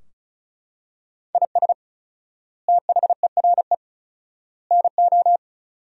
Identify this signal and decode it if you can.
Morse code, heard as a keyed tone, about 34 wpm, 700 Hz. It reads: IS THERE NO
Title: 4x Word-Spacing - Sentences from Top 200 Words - 35wpm